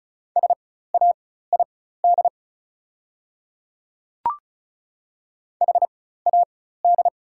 Morse code, heard as a keyed tone, about 36 wpm, spaced as for 16 wpm, 690 Hz. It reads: SAID HAD
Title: 4x Character Spacing - Top 100 Words - 35wpm